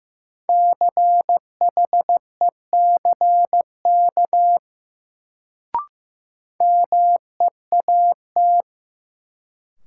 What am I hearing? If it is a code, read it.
Morse code, heard as a keyed tone, about 15 wpm, 700 Hz. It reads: CHECK MEAT
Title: Sentences from Top 1000 Words - Review Part 3 - New Word Drill - 15wpm